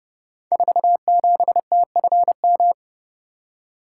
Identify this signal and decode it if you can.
Morse code, heard as a keyed tone, about 30 wpm, 700 Hz. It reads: 47TFM
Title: Code Groups - 5 Letter:Numbers - 30wpm